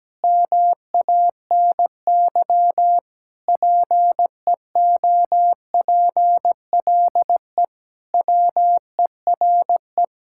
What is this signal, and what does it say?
Morse code, heard as a keyed tone, about 17 wpm, 700 Hz. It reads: MANY PEOPLE WERE
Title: Sentences from Top 1000 Words - Review Part 5 - 17wpm